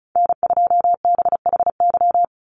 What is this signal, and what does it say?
Morse code, heard as a keyed tone, about 34 wpm, 700 Hz. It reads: N2BHY
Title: USA Call Signs 35wpm